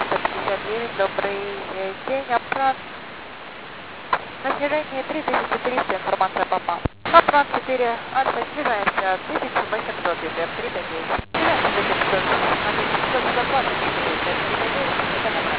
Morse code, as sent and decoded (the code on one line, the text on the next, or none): none